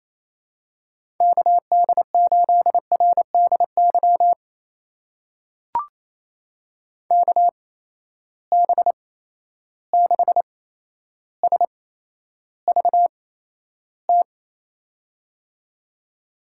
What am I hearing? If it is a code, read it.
Morse code, heard as a keyed tone, about 28 wpm, spaced as for 12 wpm, 700 Hz. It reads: KD8RDY KB6SVT
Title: USA Call Signs - 6 Characters - 8x Character Spacing - 28wpm